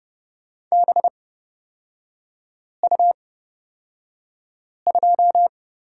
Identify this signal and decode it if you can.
Morse code, heard as a keyed tone, about 30 wpm, spaced as for 5 wpm, 700 Hz. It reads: BU2
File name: Lesson 36e - BT - Three Characters 30wpm